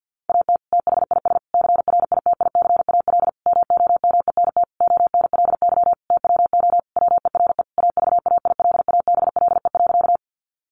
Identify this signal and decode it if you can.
Morse code, heard as a keyed tone, about 50 wpm, 700 Hz. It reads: AT THIS CRITICAL MOMENT ONLY TWO WERE AVAILABLE.
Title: Sentences from Top 1100 Words 50wpm